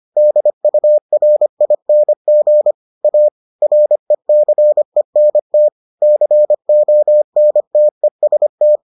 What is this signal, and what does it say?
Morse code, heard as a keyed tone, about 25 wpm, 600 Hz. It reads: DURING A RECENT CONTEST